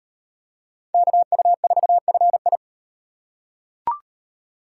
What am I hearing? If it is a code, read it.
Morse code, heard as a keyed tone, about 38 wpm, 700 Hz. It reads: KU4FI